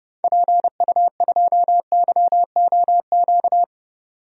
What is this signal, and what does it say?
Morse code, heard as a keyed tone, about 30 wpm, 700 Hz. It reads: PU2YOQ
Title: DXCC-100 Call Signs - 6 Characters - 30wpm